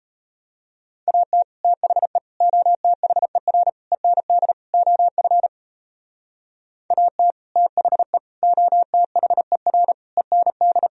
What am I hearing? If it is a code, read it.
Morse code, heard as a keyed tone, about 32 wpm, 700 Hz. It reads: AT THE OTHER END OF AT THE OTHER END
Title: Speed-Racing - Sets of 5 Words 28wpm